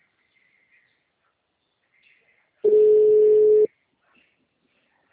Ringback tone (Europe, 425 Hz).
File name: out- -2013- - - .614